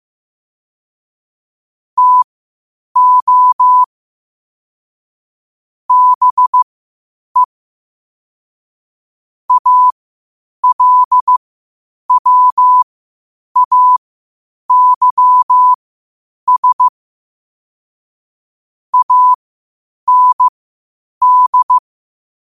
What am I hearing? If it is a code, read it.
Morse code, heard as a keyed tone, about 15 wpm, spaced as for 8 wpm, 1000 Hz. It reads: TO BE ALWAYS AND
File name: qotdfiles / 8wpm